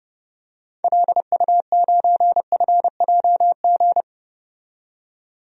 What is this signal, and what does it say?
Morse code, heard as a keyed tone, about 30 wpm, 700 Hz. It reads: LU9FJG
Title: DXCC-100 Call Signs - 6 Characters - 30wpm